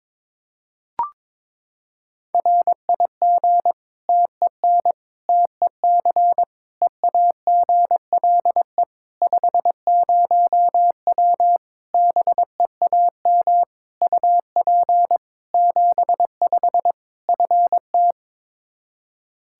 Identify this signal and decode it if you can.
Morse code, heard as a keyed tone, about 22 wpm, 700 Hz. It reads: RIG TEN TEC EAGLE 50W BEAM UP 75 FT
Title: Practice QSO - Exchange 2 - Info - 22wpm